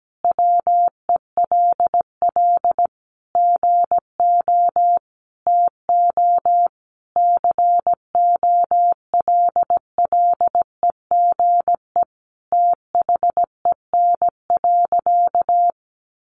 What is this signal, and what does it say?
Morse code, heard as a keyed tone, about 17 wpm, 700 Hz. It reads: WELL GO TO COLLEGE THEN.
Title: Sentences from Top 1300 Words 17wpm